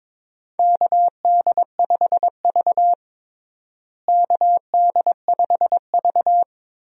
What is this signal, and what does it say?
Morse code, heard as a keyed tone, about 22 wpm, 700 Hz. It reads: KD5V KD5V
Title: USA Call Signs - 4 Characters - Repeated 3x - 22wpm